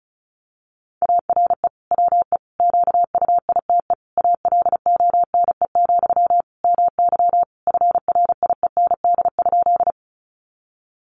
Morse code, heard as a keyed tone, about 34 wpm, 700 Hz. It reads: ARE WE QUITE ALONE, MY FRIEND?
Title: Sentences from Top 1000 Words 35wpm